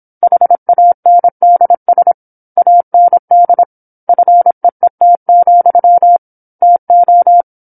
Morse code, heard as a keyed tone, about 27 wpm, 700 Hz. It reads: HANDS AND FEET, TO
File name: 08-Chapter-08 26 WPM